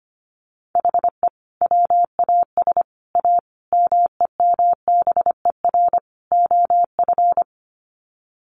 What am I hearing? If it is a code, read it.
Morse code, heard as a keyed tone, about 25 wpm, 700 Hz. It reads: HE WAS A MEMBER OF
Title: Sets of 5 Words Encore2 25wpm